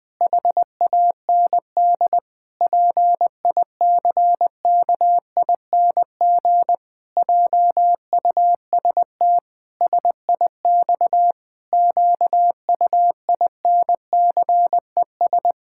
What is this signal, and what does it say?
Morse code, heard as a keyed tone, about 20 wpm, 700 Hz. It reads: HAND PICKING JUST SIX QUINCES
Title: Pangram Challenge v2 - 20wpm